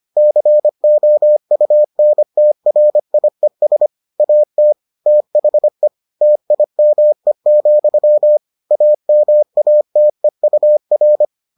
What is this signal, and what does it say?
Morse code, heard as a keyed tone, about 25 wpm, 600 Hz. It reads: COUNTRIES AT THE TIME, AMATEUR